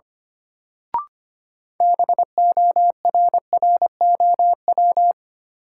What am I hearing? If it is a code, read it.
Morse code, heard as a keyed tone, about 25 wpm, 700 Hz. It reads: BORROW
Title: Sentences from Top 1500 Words - New Word Drill Alt - 25wpm